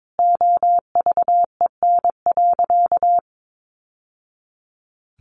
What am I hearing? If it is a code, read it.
Morse code, heard as a keyed tone, about 22 wpm, 700 Hz. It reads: OVEN.
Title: Sentences from Top 1200 Words 22wpm